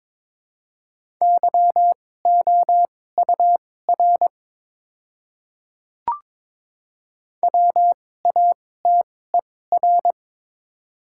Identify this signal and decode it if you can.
Morse code, heard as a keyed tone, about 22 wpm, spaced as for 16 wpm, 700 Hz. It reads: YOUR WATER
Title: 2x Character Spacing - Top 100 Words - 22wpm